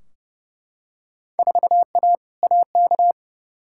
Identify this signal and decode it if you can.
Morse code, heard as a keyed tone, about 29 wpm, 700 Hz. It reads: 4A AK
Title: Field Day Exchanges Encore - 30wpm